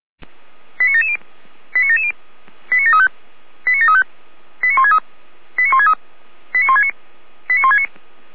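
Five-tone selective call (ZVEI), heard as quick runs of five tones, tones of about 80 ms. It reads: 89770 89770 89735 89735 89163 89163 89169 89169
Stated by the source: ZVEI five-tone call received on SDR.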